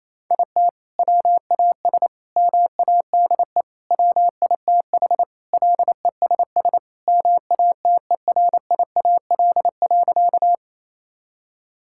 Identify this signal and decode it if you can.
Morse code, heard as a keyed tone, about 28 wpm, 700 Hz. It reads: IT WAS MADE WITH LESS MATERIAL.
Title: Sentences from Top 1000 Words - Review Part 5 - 28wpm